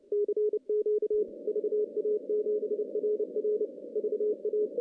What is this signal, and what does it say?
Morse code, heard as a keyed tone, about 29 wpm, 420 Hz. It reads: CQ VA7RR VA
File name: contests / CW-SS-k5tr- -0800